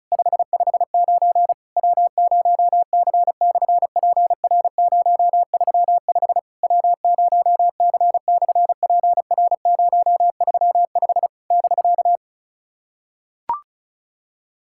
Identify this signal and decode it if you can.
Morse code, heard as a keyed tone, about 33 wpm, 700 Hz. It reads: 559 W0C/PR035 W0C/PR035 <BK>
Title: SOTA QSO Practice 35wpm